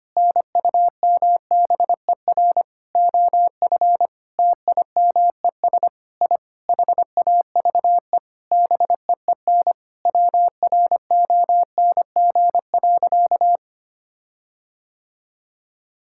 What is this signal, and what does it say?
Morse code, heard as a keyed tone, about 25 wpm, 700 Hz. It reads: NUMBER OF TIMES I HAVE BEEN WRONG.
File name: practice / 25wpm / Sentences with 9-Word Phrases - 25wpm